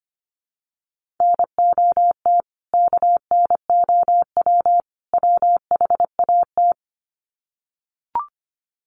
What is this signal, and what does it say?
Morse code, heard as a keyed tone, about 25 wpm, 700 Hz. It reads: NOT KNOW WHAT
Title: Sets of 3 Words 25wpm